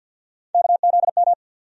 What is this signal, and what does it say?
Morse code, heard as a keyed tone, about 48 wpm, 700 Hz. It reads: KZM
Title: USA Call Sign Postfixes - 3 Letters 50wpm